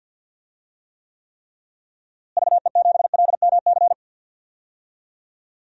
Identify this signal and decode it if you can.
Morse code, heard as a keyed tone, about 48 wpm, 700 Hz. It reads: VE7LMC
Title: DXCC-100 Call Signs - 6 Characters - Repeated 3x - 50wpm